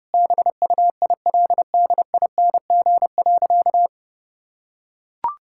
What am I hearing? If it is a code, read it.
Morse code, heard as a keyed tone, about 30 wpm, 700 Hz. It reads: BUILDING.
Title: Sentences from Top 1000 Words - Review Part 5 - 30wpm